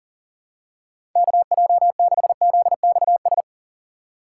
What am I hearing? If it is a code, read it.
Morse code, heard as a keyed tone, about 38 wpm, 700 Hz. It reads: KJ6ZXS
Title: USA Call Signs - 3x Character Spacing - 40wpm